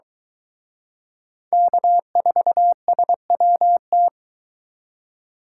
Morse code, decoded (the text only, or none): K4SWT